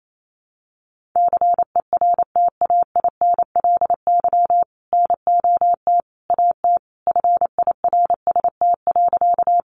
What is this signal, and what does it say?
Morse code, heard as a keyed tone, about 28 wpm, 700 Hz. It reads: CERTAINLY NOT AT FIRST.
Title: Sentences from Top 900 Words 28wpm